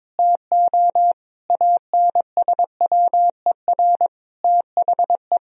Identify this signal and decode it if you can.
Morse code, heard as a keyed tone, about 22 wpm, 700 Hz. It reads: TO ANSWER THE
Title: Sentences from Top 1000 Words - Review Part 1 - 22wpm